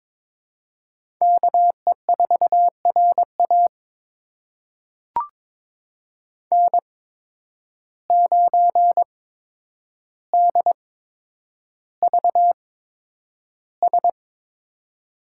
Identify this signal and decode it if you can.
Morse code, heard as a keyed tone, about 22 wpm, spaced as for 10 wpm, 700 Hz. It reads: KE4RA N9DVS
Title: USA Call Signs - 5 Characters - 8x Character Spacing - 22wpm